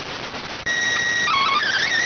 SSTV (slow-scan television), PD 290.